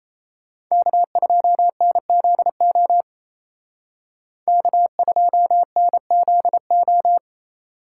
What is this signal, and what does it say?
Morse code, heard as a keyed tone, about 36 wpm, 700 Hz. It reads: K2NZO K2NZO